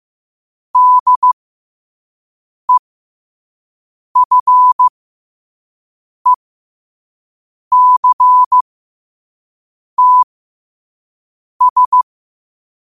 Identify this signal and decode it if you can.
Morse code, heard as a keyed tone, about 15 wpm, spaced as for 5 wpm, 1000 Hz. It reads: DEFECTS
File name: qotdfiles / 5wpm